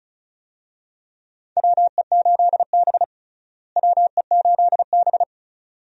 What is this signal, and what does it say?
Morse code, heard as a keyed tone, about 36 wpm, 700 Hz. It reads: WE8B WE8B
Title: USA Call Signs - 4 Characters - Repeated 2x - 35wpm